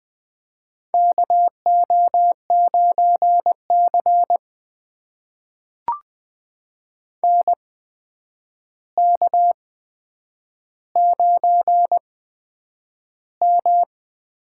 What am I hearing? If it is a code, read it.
Morse code, heard as a keyed tone, about 20 wpm, spaced as for 9 wpm, 700 Hz. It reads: KO9C NK9M
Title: USA Call Signs - 8x Character Spacing - 20wpm